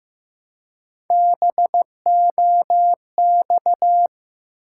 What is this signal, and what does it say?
Morse code, heard as a keyed tone, about 15 wpm, 700 Hz. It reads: BOX
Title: Three Letter Words - 15wpm